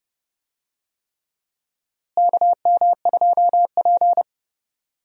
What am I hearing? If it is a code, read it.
Morse code, heard as a keyed tone, about 30 wpm, 700 Hz. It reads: KM2P
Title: USA Call Signs - 4 Characters - Repeated 3x - 30wpm